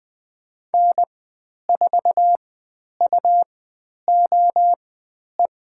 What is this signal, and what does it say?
Morse code, heard as a keyed tone, about 20 wpm, spaced as for 10 wpm, 700 Hz. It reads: N4UOE
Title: 109 Lesson 18g - Letter m - Call Signs 20wpm